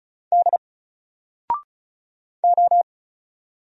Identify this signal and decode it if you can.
Morse code, heard as a keyed tone, about 33 wpm, 700 Hz. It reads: D O
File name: Single-Letter-Number-35wpm